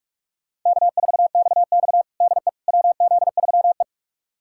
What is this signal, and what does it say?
Morse code, heard as a keyed tone, about 43 wpm, 700 Hz. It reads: K4XX DE WZ3E
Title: USA Callsign DE Callsign - 4 Characters - 45wpm